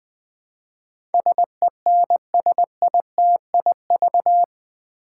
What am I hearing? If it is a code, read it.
Morse code, heard as a keyed tone, about 20 wpm, 700 Hz. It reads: SENSITIV
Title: Build A Word - 4 Letters At A Time - 11-Letter Words - 20wpm